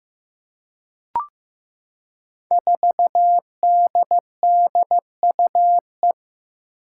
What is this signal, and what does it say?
Morse code, heard as a keyed tone, about 15 wpm, 700 Hz. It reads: 4DDUE
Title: Five Challenging Letter-Number Combinations 15wpm